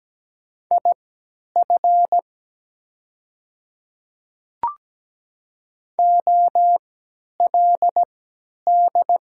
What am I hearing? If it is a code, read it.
Morse code, heard as a keyed tone, about 17 wpm, spaced as for 10 wpm, 700 Hz. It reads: IF OLD
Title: 3x Character Spacing - Top 200 Words - 17wpm